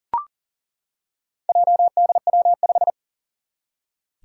Morse code, heard as a keyed tone, about 39 wpm, 700 Hz. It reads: JDW5
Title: Four Challenging Letter-Number Combinations 40wpm